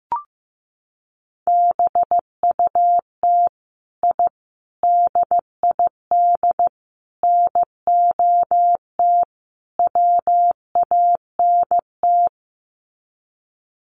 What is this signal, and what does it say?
Morse code, heard as a keyed tone, about 15 wpm, 700 Hz. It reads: BUT I DID NOT WANT
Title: Sets of 5 Words Encore 15wpm